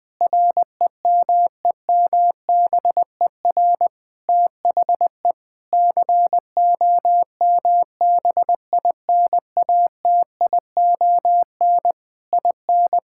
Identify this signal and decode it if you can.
Morse code, heard as a keyed tone, about 21 wpm, 700 Hz. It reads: REMEMBER THE COMBINATION IN
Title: Sentences from Top 1000 Words - Review Part 6 - 20wpm